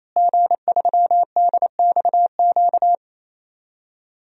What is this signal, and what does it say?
Morse code, heard as a keyed tone, about 28 wpm, 700 Hz. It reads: G3DXQ
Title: DXCC-100 Call Signs - 5 Characters - Repeated 3x - 28wpm